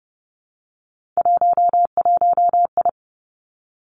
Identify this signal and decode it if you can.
Morse code, heard as a keyed tone, about 30 wpm, 700 Hz. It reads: 11I
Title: Three Challenging Letter-Number Combinations 30wpm